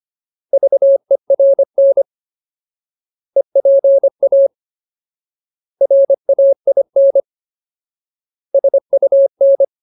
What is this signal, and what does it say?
Morse code, heard as a keyed tone, about 25 wpm, 550 Hz. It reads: VERN EPA RAIN SUN
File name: Ultimate Word List / 25WPM Ultimate